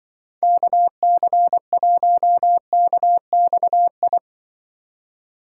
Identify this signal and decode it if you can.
Morse code, heard as a keyed tone, about 24 wpm, 700 Hz. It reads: KC1KXI
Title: Speed Racing - USA 2x3 Call Signs 20wpm